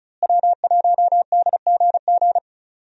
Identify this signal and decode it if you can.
Morse code, heard as a keyed tone, about 34 wpm, 700 Hz. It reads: W1DGG
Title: USA Call Signs - 5 Characters - Repeated 3x - 35wpm